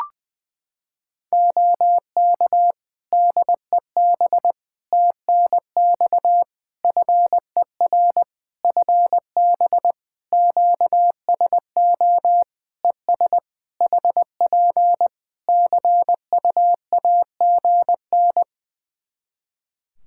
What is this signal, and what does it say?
Morse code, heard as a keyed tone, about 21 wpm, 700 Hz. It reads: OK DEB TNX FER FB QSO ES HP CUAGN
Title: Practice QSO - Exchange 4 - Ending - 20wpm